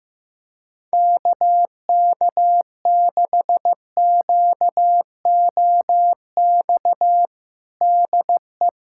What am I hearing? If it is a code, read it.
Morse code, heard as a keyed tone, about 16 wpm, 700 Hz. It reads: KK6QOX DE
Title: USA Callsign DE Callsign - 6 Characters - 15wpm